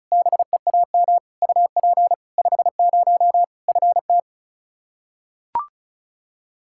Morse code, heard as a keyed tone, about 34 wpm, 700 Hz. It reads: BEAM UP 50 FT
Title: Practice QSO - Exchange 2 - Info - 35wpm